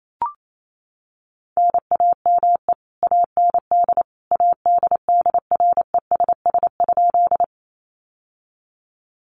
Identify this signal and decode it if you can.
Morse code, heard as a keyed tone, about 28 wpm, 700 Hz. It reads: NAME AND ADDRESS?
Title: Sentences from Top 1000 Words 28wpm